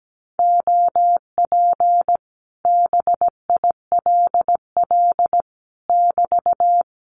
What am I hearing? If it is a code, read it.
Morse code, heard as a keyed tone, about 17 wpm, 700 Hz. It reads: OP BILL <BT>
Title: Common QSO Phrases Encore 17wpm